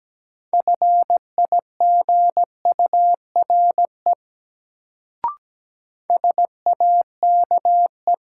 Morse code, heard as a keyed tone, about 17 wpm, 700 Hz. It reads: FIGURE SAKE
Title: Sets of 4 Words - New Word Drill - 17wpm